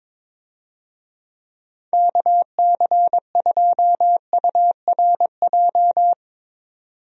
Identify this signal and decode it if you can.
Morse code, heard as a keyed tone, about 22 wpm, 700 Hz. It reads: KC2URJ
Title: USA Call Signs - 6 Characters - 22wpm